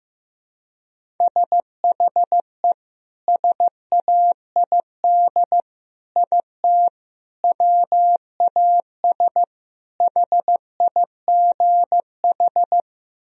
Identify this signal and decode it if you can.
Morse code, heard as a keyed tone, about 16 wpm, 700 Hz. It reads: SHE SAID IT WAS HIGH
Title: Sentences from Top 1000 Words - Review Part 1 - 15wpm